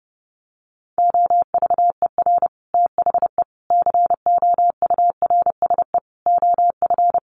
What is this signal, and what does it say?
Morse code, heard as a keyed tone, about 30 wpm, 700 Hz. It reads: OVER THE COURSE OF
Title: Sets of 5 Words 30wpm